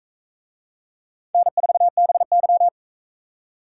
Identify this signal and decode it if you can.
Morse code, heard as a keyed tone, about 39 wpm, 700 Hz. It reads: N4BY